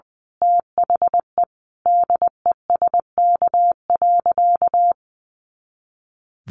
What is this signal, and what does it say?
Morse code, heard as a keyed tone, about 20 wpm, 700 Hz. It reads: THE DESK.
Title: A Sound Of Thunder 20wpm